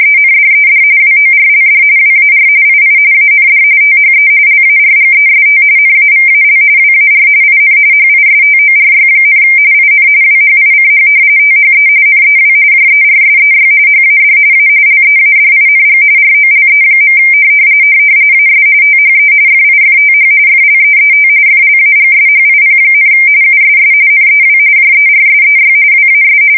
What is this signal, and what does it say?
RTTY, about 45.45 baud. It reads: ENDED WORLD WAR II AND THE LARGEST NAVAL WAR IN HISTORY, IN WHICH THE NEW JERSEY HAD A LEADING ROLE. AS THE FIFTH FLEET FLAGSHIP, USS NEW JERSEY WAS
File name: fsk170 / battleshipnj